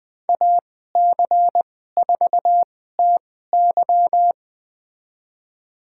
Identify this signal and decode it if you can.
Morse code, heard as a keyed tone, about 20 wpm, spaced as for 14 wpm, 700 Hz. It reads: AC4TY
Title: USA Call Signs - 5 Characters - 2x Character Spacing - 20wpm